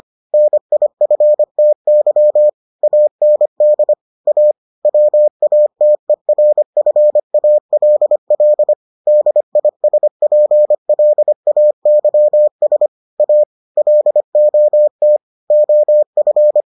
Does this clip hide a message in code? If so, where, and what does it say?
Yes: Morse code from 0.3 to 16.6 s, heard as a keyed tone, about 25 wpm, 600 Hz. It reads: NIFTY AND A WATERFALL DISPLAYS A LOT OF